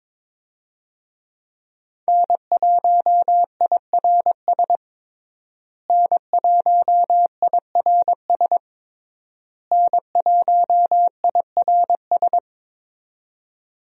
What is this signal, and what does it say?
Morse code, heard as a keyed tone, about 21 wpm, 700 Hz. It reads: N1IRS N1IRS N1IRS
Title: USA Call Signs - 5 Characters - Repeated 3x - 22wpm